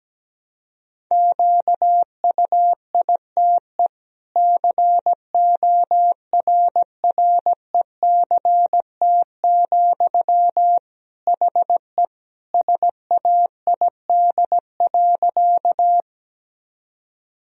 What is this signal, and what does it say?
Morse code, heard as a keyed tone, about 17 wpm, 700 Hz. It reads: QUITE CORRECT, HE SAID.
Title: Sentences from Top 1000 Words - Review Part 2 - 17wpm